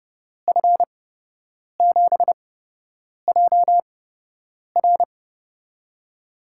Morse code, heard as a keyed tone, about 30 wpm, spaced as for 8 wpm, 700 Hz. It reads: F7JR
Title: USA Call Signs - 8x Character Spacing - 30wpm